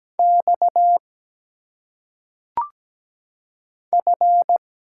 Morse code, heard as a keyed tone, about 17 wpm, 700 Hz. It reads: X F
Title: Single Letters - Phonetic Pronunciation 17wpm